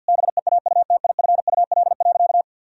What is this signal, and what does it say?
Morse code, heard as a keyed tone, about 50 wpm, 700 Hz. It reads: BEAUTIFUL.
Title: Sentences from Top 1000 Words - Review Part 1 - 50wpm